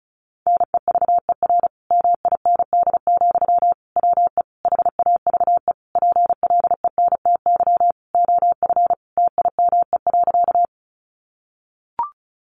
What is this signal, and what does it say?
Morse code, heard as a keyed tone, about 34 wpm, 700 Hz. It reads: NEVER MIND, WE HAVE PLENTY OF TIME.
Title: Sentences from Top 1300 Words 35wpm